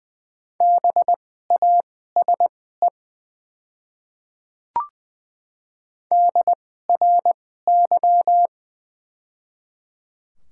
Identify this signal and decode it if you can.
Morse code, heard as a keyed tone, about 20 wpm, spaced as for 15 wpm, 700 Hz. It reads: BASE DRY
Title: 2x Character Spacing - Top 500 Words - 20wpm